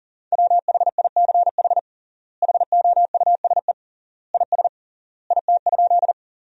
Morse code, heard as a keyed tone, about 38 wpm, 700 Hz. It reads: WHICH HOUSE IS IT?
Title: 3x Word-Spacing - Sentences from Top 200 Words - 40wpm